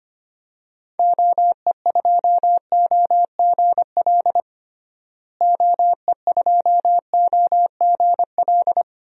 Morse code, heard as a keyed tone, about 25 wpm, 700 Hz. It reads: OE2OGL OE2OGL
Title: DXCC-100 Call Signs - 6 Characters - Repeated 3x - 25wpm